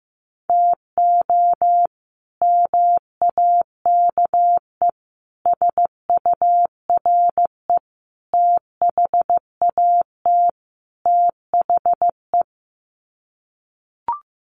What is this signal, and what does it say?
Morse code, heard as a keyed tone, about 15 wpm, 700 Hz. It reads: TO MAKE SURE THAT THE
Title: Sets of 5 Words 15wpm